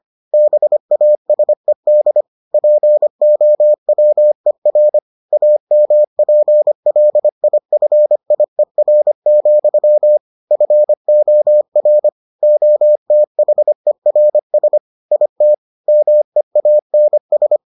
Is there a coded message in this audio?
Yes: Morse code from 0.3 to 17.6 s, heard as a keyed tone, about 25 wpm, 600 Hz. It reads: BASED POWER AMPLIFIER, FOR OTHERS IT MEANS